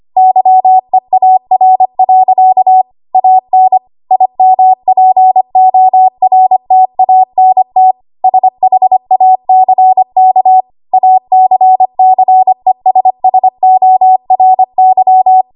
Morse code, heard as a keyed tone, about 25 wpm, 750 Hz. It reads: YEAR. AN IMPORTANT SHACK ACCESSORY
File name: Morse / 25WPM